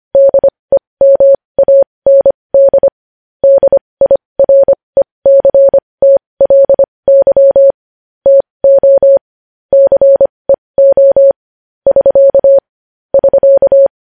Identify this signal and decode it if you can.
Morse code, heard as a keyed tone, about 25 wpm, spaced as for 20 wpm, 550 Hz. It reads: DEMAND DIRECTLY TO CEO <SK> <SK>
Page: news-headlines-in-morse-code-at-20-wpm